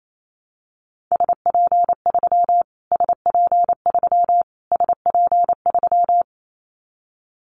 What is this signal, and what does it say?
Morse code, heard as a keyed tone, about 28 wpm, 700 Hz. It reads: SP3 SP3 SP3
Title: Callsign Prefixes for Top 25 DXCC - Repeated 3x - 28wpm